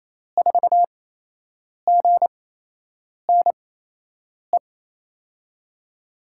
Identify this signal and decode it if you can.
Morse code, heard as a keyed tone, about 28 wpm, spaced as for 8 wpm, 700 Hz. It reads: 4GNE
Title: USA Call Signs - 6 Characters - 8x Character Spacing - 28wpm